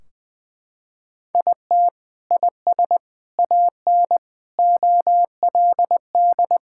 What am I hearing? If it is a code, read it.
Morse code, heard as a keyed tone, about 21 wpm, 700 Hz. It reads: IT IS AN OLD
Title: Sentences from Top 1600 Words 20wpm